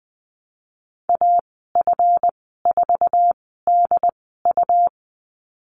Morse code, heard as a keyed tone, about 20 wpm, spaced as for 14 wpm, 700 Hz. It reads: AF4DU
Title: 154 Lesson 25g - Letter p - Call Signs 20wpm